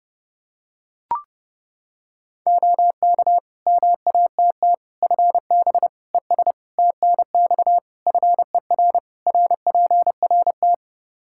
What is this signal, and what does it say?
Morse code, heard as a keyed tone, about 30 wpm, 700 Hz. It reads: OK MATT FB ES TNX FER RPRT
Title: Practice QSO - Exchange 2 - Info - 30wpm